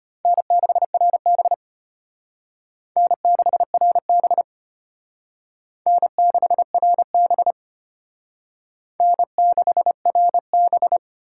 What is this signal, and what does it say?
Morse code, heard as a keyed tone, about 32 wpm, 700 Hz. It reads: N6RB N6RB N6RB N6RB